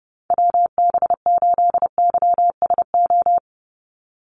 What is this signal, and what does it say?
Morse code, heard as a keyed tone, about 30 wpm, 700 Hz. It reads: WB8YSO
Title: USA Call Signs 30wpm